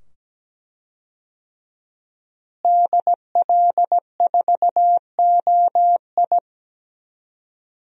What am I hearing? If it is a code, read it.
Morse code, heard as a keyed tone, about 17 wpm, 700 Hz. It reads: DL4OI
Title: USA Call Signs - 5 Characters - Repeated 2x - 17wpm